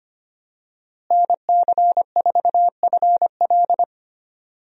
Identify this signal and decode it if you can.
Morse code, heard as a keyed tone, about 25 wpm, 700 Hz. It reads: NC4FL
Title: USA Call Signs - 5 Characters - Repeated 2x - 25wpm